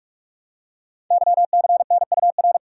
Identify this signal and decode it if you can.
Morse code, heard as a keyed tone, about 44 wpm, 700 Hz. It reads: YCNUR
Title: Code Groups - 5 Letter:Numbers - 45wpm